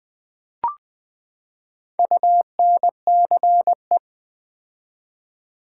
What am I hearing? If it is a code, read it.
Morse code, heard as a keyed tone, about 20 wpm, 700 Hz. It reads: UNCE
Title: Build A Word - 4 Letters At A Time - 11-Letter Words - 20wpm